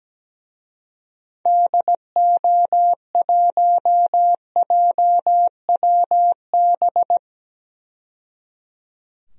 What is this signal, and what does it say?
Morse code, heard as a keyed tone, about 17 wpm, 700 Hz. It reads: DO1JWB
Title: DXCC-100 Call Signs - 6 Characters - 17wpm